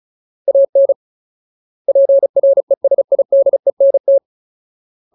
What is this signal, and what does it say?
Morse code, heard as a keyed tone, about 34 wpm, 550 Hz. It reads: AN PRESIDENT
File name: Ultimate Word List / 35WPM Ultimate